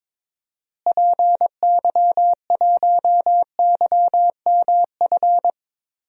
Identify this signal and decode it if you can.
Morse code, heard as a keyed tone, about 21 wpm, 700 Hz. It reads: PY1YMF
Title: DXCC-100 Call Signs - 6 Characters - Repeated 2x - 22wpm